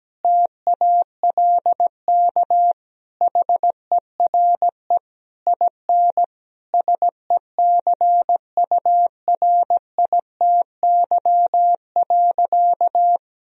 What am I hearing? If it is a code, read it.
Morse code, heard as a keyed tone, about 17 wpm, 700 Hz. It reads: TALK HERE IN SECURITY.
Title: Sentences from Top 1000 Words - Review Part 4 - 17wpm